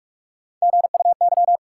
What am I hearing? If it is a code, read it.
Morse code, heard as a keyed tone, about 46 wpm, 700 Hz. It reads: GUY
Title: Three Letter Words - 45wpm